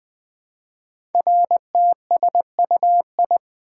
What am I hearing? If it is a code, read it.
Morse code, heard as a keyed tone, about 20 wpm, 700 Hz. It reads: RTSUI
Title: Code Groups - 5 Letters - 20wpm